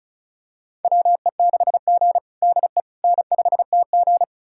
Morse code, heard as a keyed tone, about 36 wpm, 700 Hz. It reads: WE6G DE N5TG
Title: USA Callsign DE Callsign - 4 Characters - 35wpm